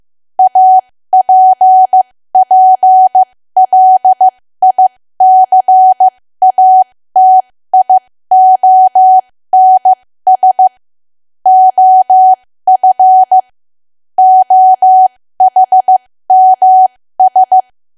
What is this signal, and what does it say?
Morse code, heard as a keyed tone, about 15 wpm, spaced as for 13 wpm, 750 Hz. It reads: APPLICATIONS OF OHMS
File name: cwa / W1AW-Level2 / 13WPM